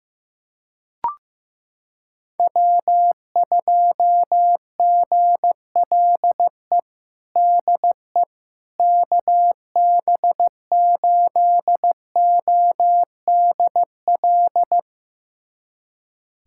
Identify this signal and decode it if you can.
Morse code, heard as a keyed tone, about 16 wpm, 700 Hz. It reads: W2GLE DE KB8ODL
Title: USA Callsign DE Callsign 15wpm